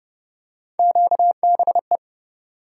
Morse code, heard as a keyed tone, about 29 wpm, 700 Hz. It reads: QBE
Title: USA Call Sign Postfixes - 3 Letters 30wpm